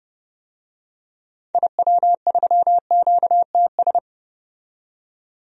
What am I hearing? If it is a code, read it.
Morse code, heard as a keyed tone, about 30 wpm, 700 Hz. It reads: IW3QTS
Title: DXCC-100 Call Signs - 6 Characters - 30wpm